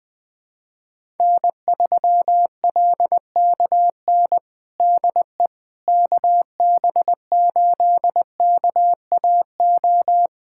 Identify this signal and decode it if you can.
Morse code, heard as a keyed tone, about 20 wpm, 700 Hz. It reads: N3LKN DE KB8KAO
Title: USA Callsign DE Callsign 20wpm